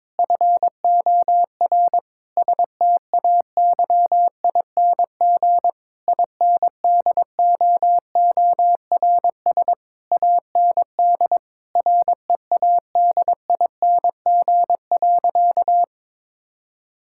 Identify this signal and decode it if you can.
Morse code, heard as a keyed tone, about 22 wpm, 700 Hz. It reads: FOR STAYING INDOORS AND READING.
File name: Difficult Sentences with Hints - 22wpm